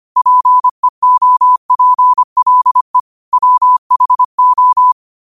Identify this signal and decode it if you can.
Morse code, heard as a keyed tone, about 25 wpm, 1000 Hz. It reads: PEOPLE WHO
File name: qotdfiles / 25wpm